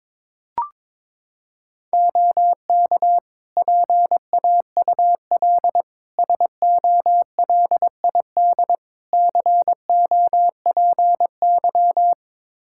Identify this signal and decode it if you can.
Morse code, heard as a keyed tone, about 22 wpm, 700 Hz. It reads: OK PAUL SOLID COPY
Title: K1USN SST QSO Practice 22wpm